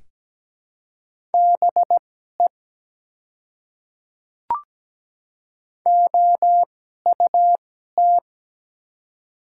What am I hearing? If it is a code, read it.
Morse code, heard as a keyed tone, about 17 wpm, spaced as for 12 wpm, 690 Hz. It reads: BE OUT
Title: 2x Character Spacing - Top 100 Words - 17wpm